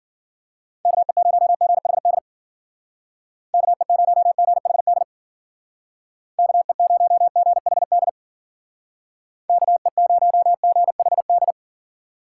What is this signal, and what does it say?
Morse code, heard as a keyed tone, about 54 wpm, 700 Hz. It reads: KE0GHD KE0GHD KE0GHD KE0GHD